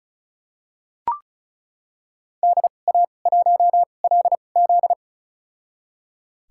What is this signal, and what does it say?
Morse code, heard as a keyed tone, about 36 wpm, spaced as for 25 wpm, 700 Hz. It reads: DA1LZ
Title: USA Call Signs - 5 Characters - 2x Character Spacing - 35wpm